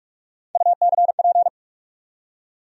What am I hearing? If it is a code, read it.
Morse code, heard as a keyed tone, about 41 wpm, 700 Hz. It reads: UCP
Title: USA Call Sign Postfixes - 3 Letters 45wpm